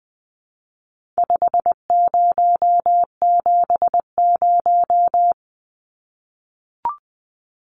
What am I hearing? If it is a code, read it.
Morse code, heard as a keyed tone, about 20 wpm, 700 Hz. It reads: 5070
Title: Numbers - 4-Digit 20wpm